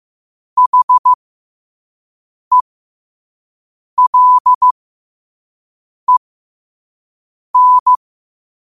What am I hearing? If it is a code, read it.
Morse code, heard as a keyed tone, about 15 wpm, spaced as for 5 wpm, 1000 Hz. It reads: HELEN
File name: qotdfiles / 5wpm